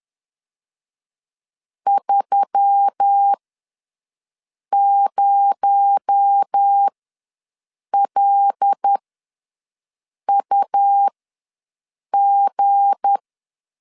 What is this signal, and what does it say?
Morse code, heard as a keyed tone, about 10 wpm, spaced as for 5 wpm, 800 Hz. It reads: 30LUG
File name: LEZIONI / 9 28 ws 10wpm 4124